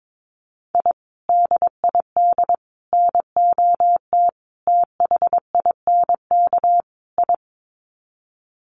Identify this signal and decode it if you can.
Morse code, heard as a keyed tone, about 22 wpm, 700 Hz. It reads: I DID NOT THINK I
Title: Sets of 5 Words Encore 22wpm